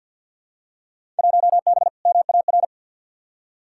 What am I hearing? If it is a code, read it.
Morse code, heard as a keyed tone, about 52 wpm, 700 Hz. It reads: 1B MAR
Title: Field Day Exchanges Encore - 50wpm